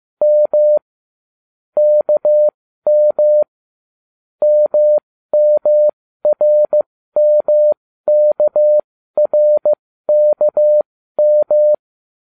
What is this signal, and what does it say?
Morse code, heard as a keyed tone, about 15 wpm, spaced as for 13 wpm, 600 Hz. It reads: M KM MMRMKRKM